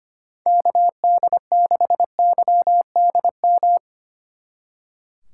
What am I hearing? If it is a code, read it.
Morse code, heard as a keyed tone, about 32 wpm, 700 Hz. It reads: KD6YDM